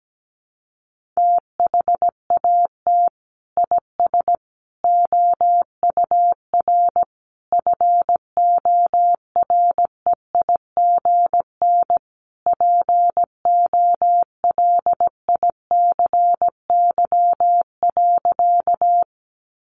Morse code, heard as a keyed tone, about 17 wpm, 700 Hz. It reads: THAT IS OUR FOREIGN POLICY.
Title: Sentences from Top 1400 Words 17wpm